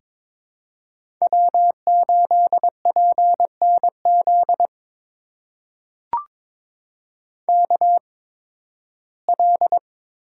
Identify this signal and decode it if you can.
Morse code, heard as a keyed tone, about 22 wpm, spaced as for 14 wpm, 700 Hz. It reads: W8PNZ KL
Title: USA Call Signs - 5 Characters - 8x Character Spacing - 22wpm